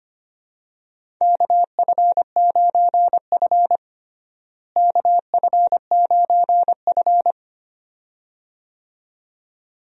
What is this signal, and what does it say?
Morse code, heard as a keyed tone, about 25 wpm, 700 Hz. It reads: KF9F KF9F
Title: USA Call Signs - 4 Characters - Repeated 2x - 25wpm